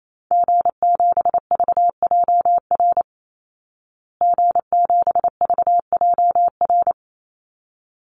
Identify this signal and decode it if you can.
Morse code, heard as a keyed tone, about 28 wpm, 700 Hz. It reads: G7VJR G7VJR
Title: Call Signs Repeated 3x - 28wpm